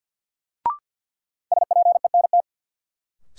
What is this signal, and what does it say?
Morse code, heard as a keyed tone, about 49 wpm, 700 Hz. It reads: SPENT
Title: Sets of 5 Words - New Word Drill - 50wpm